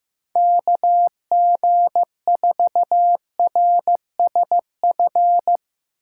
Morse code, heard as a keyed tone, about 17 wpm, 700 Hz. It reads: KG4RSF
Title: Speed Racing - USA 2x3 Call Signs 15wpm